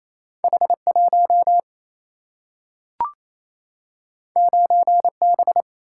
Morse code, heard as a keyed tone, about 28 wpm, 700 Hz. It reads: H1 9B
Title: Single-Letter Single-Number 28wpm